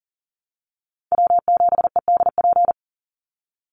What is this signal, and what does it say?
Morse code, heard as a keyed tone, about 40 wpm, 700 Hz. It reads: W7EDP
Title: USA Call Signs 40wpm